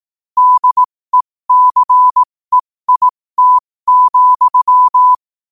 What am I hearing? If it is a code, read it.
Morse code, heard as a keyed tone, about 18 wpm, spaced as for 15 wpm, 1000 Hz. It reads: DECEIT,
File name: qotdfiles / 15wpm